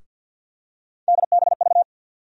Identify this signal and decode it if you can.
Morse code, heard as a keyed tone, about 48 wpm, 700 Hz. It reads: DBV
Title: USA Call Sign Postfixes - 3 Letters 50wpm